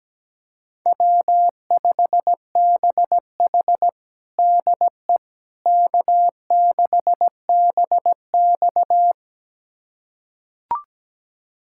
Morse code, heard as a keyed tone, about 17 wpm, 700 Hz. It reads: W5BH DE K6BX
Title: USA Callsign DE Callsign - 4 Characters - 17wpm